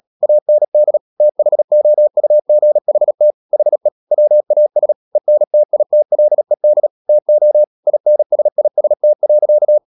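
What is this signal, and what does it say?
Morse code, heard as a keyed tone, about 37 wpm, 600 Hz. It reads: AND THOUGHT HE WAS ENTITLED TO INSIST.
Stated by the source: Morse code practice at 35–39 wpm sending English text of Ken Follett's novel Never, from CW med Gnister.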